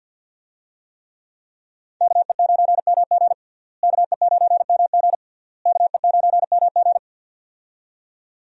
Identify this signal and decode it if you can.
Morse code, heard as a keyed tone, about 52 wpm, 700 Hz. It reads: KE9MG KE9MG KE9MG
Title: USA Call Signs - 5 Characters - Repeated 3x - 50wpm